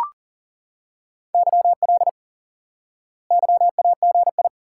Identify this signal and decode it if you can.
Morse code, heard as a keyed tone, about 40 wpm, 700 Hz. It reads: YL YAGI
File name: IQE Course 3 - Lesson 89 - YAGI - Two Elements - 40wpm